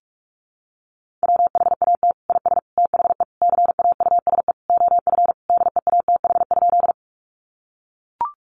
Morse code, heard as a keyed tone, about 44 wpm, 700 Hz. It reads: WHAT IS THE CAUSE OF DEATH?
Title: Sentences from Top 1100 Words 45wpm